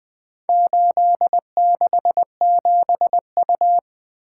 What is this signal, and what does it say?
Morse code, heard as a keyed tone, about 20 wpm, 700 Hz. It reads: 867U
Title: Four Letter-Number Combinations - 20wpm